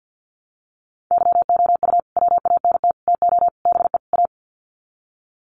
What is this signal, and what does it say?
Morse code, heard as a keyed tone, about 50 wpm, 700 Hz. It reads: YOU WANT TO BE A
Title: Sets of 5 Words Encore2 50wpm